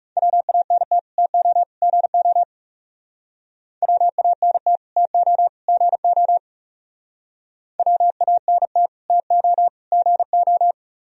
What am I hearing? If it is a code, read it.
Morse code, heard as a keyed tone, about 39 wpm, 700 Hz. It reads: WANT TO GO WANT TO GO WANT TO GO